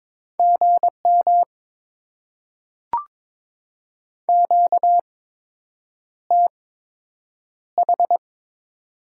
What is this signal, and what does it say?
Morse code, heard as a keyed tone, about 22 wpm, spaced as for 8 wpm, 700 Hz. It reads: GM QTH
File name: Top 20 QSO Elements - 8x Character Spacing - 22wpm